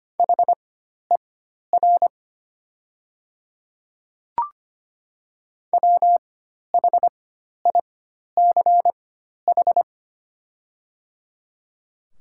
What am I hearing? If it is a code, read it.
Morse code, heard as a keyed tone, about 25 wpm, spaced as for 12 wpm, 700 Hz. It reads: HER WHICH
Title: 4x Character Spacing - Top 100 Words - 25wpm